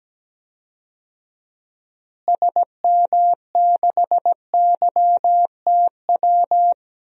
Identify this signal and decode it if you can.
Morse code, heard as a keyed tone, about 17 wpm, 700 Hz. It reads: SM6YTW
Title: DXCC-100 Call Signs - 6 Characters - Repeated 2x - 17wpm